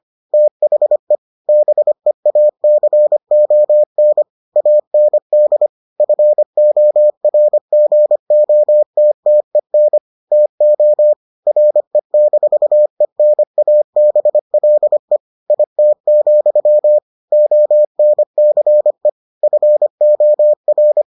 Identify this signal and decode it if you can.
Morse code, heard as a keyed tone, about 25 wpm, 600 Hz. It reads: THE BEACON AND FORGOTTEN TO RE-ENABLE IT, ONCE FOR